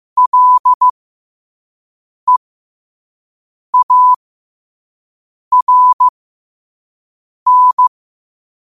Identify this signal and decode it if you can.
Morse code, heard as a keyed tone, about 15 wpm, spaced as for 5 wpm, 1000 Hz. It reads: LEARN
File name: qotdfiles / 5wpm